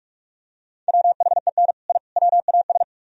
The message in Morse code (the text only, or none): WHEN I WAS